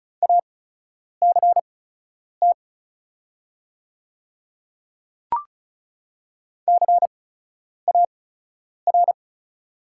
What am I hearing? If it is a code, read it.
Morse code, heard as a keyed tone, about 34 wpm, spaced as for 10 wpm, 700 Hz. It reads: ACT CAR
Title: 8x Character Spacing - Top 300 Words - 35wpm